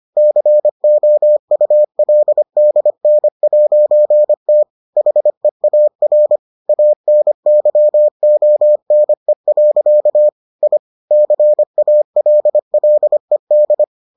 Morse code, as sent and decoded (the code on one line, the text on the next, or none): -.-. --- ..- .-.. -.. -. .----. - / .... .- .-. / .- -. -.-- --- -. .-.-.- / .. / -.-. .- .-.. .-.. -..
COULDN'T HEAR ANYONE. I CALLED